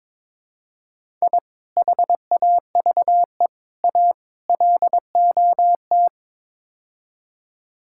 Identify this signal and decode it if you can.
Morse code, heard as a keyed tone, about 22 wpm, 700 Hz. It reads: I HAVE A LOT
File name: practice / 22wpm / Build A Phrase - 5-Word Phrases - 22wpm